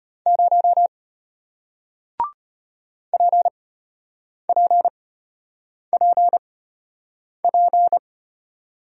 Morse code, heard as a keyed tone, about 33 wpm, spaced as for 8 wpm, 700 Hz. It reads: 0 PPPP